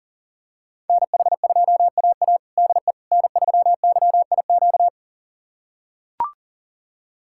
Morse code, heard as a keyed tone, about 40 wpm, 700 Hz. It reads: NH2AA DE N3YIQ
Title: USA Callsign DE Callsign - 5 Characters - 40wpm